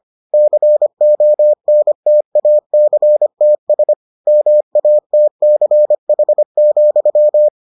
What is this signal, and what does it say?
Morse code, heard as a keyed tone, about 25 wpm, 600 Hz. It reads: CONTACTS MATCH,